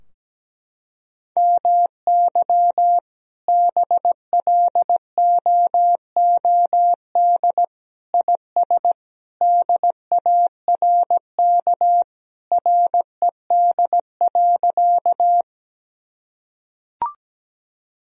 Morse code, heard as a keyed tone, about 17 wpm, 700 Hz. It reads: MY BLOOD IS DARK RED.
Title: Sentences from Top 1000 Words - Review Part 2 - 17wpm